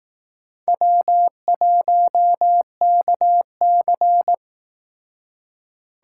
Morse code, heard as a keyed tone, about 18 wpm, 700 Hz. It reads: W1KC
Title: Speed Racing - USA 1x2 Call Signs 15wpm